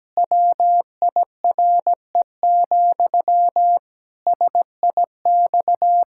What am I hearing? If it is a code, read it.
Morse code, heard as a keyed tone, about 17 wpm, 700 Hz. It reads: WIRE, SIX
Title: Pangram Challenge v2 - 17wpm